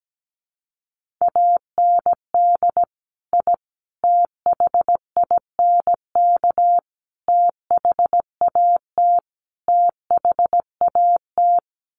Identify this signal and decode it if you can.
Morse code, heard as a keyed tone, about 17 wpm, 700 Hz. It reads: AND I THINK THAT THAT
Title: Sets of 5 Words Encore 17wpm